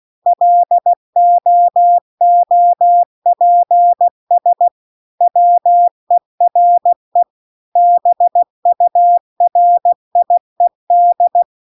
Morse code, heard as a keyed tone, about 16 wpm, 700 Hz. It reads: LOOPS WERE BURIED